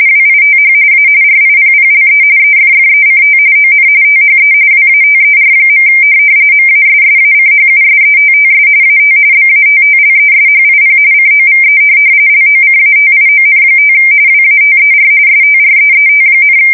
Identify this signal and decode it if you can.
RTTY, about 45.45 baud. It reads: WITH A RIPPLE EFFECT ON THE WATER. MASSACHUSETTS AT BATTLESHIP COVE LOCATION5 WATER STREET,